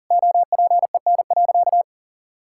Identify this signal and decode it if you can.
Morse code, heard as a keyed tone, about 40 wpm, 700 Hz. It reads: OPEN.
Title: Sentences from Top 1600 Words 40wpm